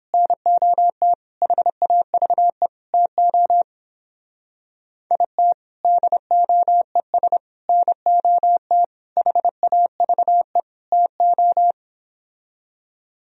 Morse code, heard as a keyed tone, about 26 wpm, 700 Hz. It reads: NOT HAVE TO IT DOES NOT HAVE TO
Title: Speed-Racing - Sets of 5 Words 22wpm